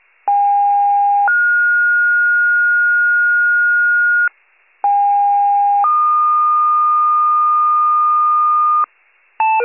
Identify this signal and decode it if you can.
Two-tone page, heard as a one-second tone, then a three-second tone, about 810 then 1430 Hz, about 810 then 1220 Hz, with noise. It is heard from 0.3 to 8.8 s.